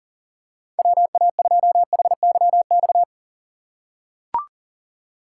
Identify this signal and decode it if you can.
Morse code, heard as a keyed tone, about 40 wpm, 700 Hz. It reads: WA2HYX
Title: USA Call Signs - 6 Characters - 6x Character Spacing - 40wpm